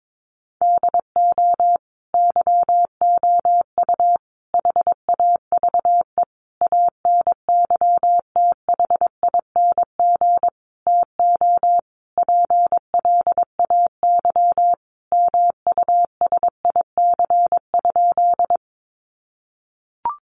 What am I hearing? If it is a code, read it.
Morse code, heard as a keyed tone, about 22 wpm, 700 Hz. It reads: DO YOU HAVE ANYTHING TO PLAY MUSIC?
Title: Sentences from Top 1000 Words 22wpm